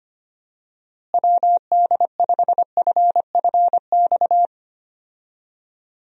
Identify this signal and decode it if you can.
Morse code, heard as a keyed tone, about 25 wpm, 700 Hz. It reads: WD5FFX
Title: USA Call Signs - 6 Characters - Repeated 3x - 25wpm